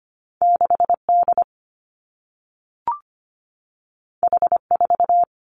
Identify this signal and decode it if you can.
Morse code, heard as a keyed tone, about 25 wpm, 700 Hz. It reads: 6 D H 4
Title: Two Challenging Letter-Number Combinations 25wpm